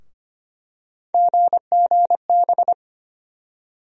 Morse code, heard as a keyed tone, about 25 wpm, 700 Hz. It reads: GGB
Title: USA Call Sign Postfixes - 3 Letters 25wpm